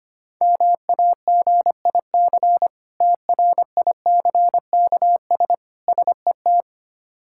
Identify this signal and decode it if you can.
Morse code, heard as a keyed tone, about 25 wpm, 700 Hz. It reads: MAGIC TRICKS SET
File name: practice / 25wpm / Difficult Words with Hints - 25wpm